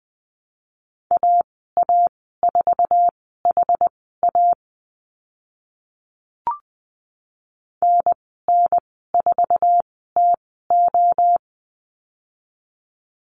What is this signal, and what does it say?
Morse code, heard as a keyed tone, about 20 wpm, spaced as for 14 wpm, 700 Hz. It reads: AA4HA NN4TO
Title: 062 Lesson 11g - Letter h - Call Signs 20wpm